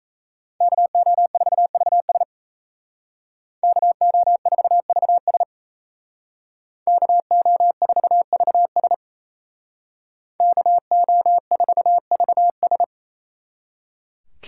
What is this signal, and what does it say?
Morse code, heard as a keyed tone, about 36 wpm, 700 Hz. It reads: KO4VS KO4VS KO4VS KO4VS